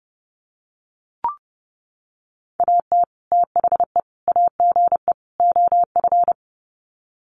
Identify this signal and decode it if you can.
Morse code, heard as a keyed tone, about 30 wpm, 700 Hz. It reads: AT THE AGE OF
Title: Sets of 4 Words 30wpm